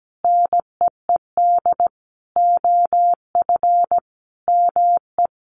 Morse code, heard as a keyed tone, about 17 wpm, 700 Hz. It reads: NEED OF ME
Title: Sentences from Top 1400 Words 17wpm